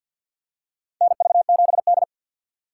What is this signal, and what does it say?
Morse code, heard as a keyed tone, about 47 wpm, 700 Hz. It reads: NV7D
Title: USA Call Signs - 4 Characters - 3x Character Spacing - 50wpm